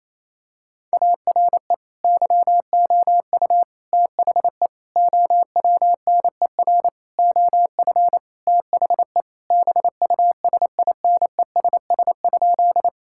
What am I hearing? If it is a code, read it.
Morse code, heard as a keyed tone, about 28 wpm, 700 Hz. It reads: ARE YOU THE OWNER OF THE BUSINESS?
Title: Sentences from Top 1600 Words 28wpm